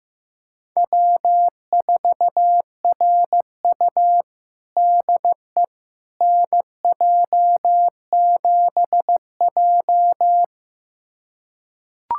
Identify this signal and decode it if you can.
Morse code, heard as a keyed tone, about 15 wpm, 700 Hz. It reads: W4RU DE NJ7J
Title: USA Callsign DE Callsign - 4 Characters - 15wpm